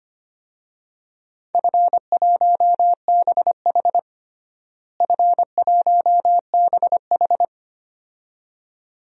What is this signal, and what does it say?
Morse code, heard as a keyed tone, about 25 wpm, 700 Hz. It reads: F1BH F1BH
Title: DXCC-100 Call Signs - 4 Characters - Repeated 2x - 25wpm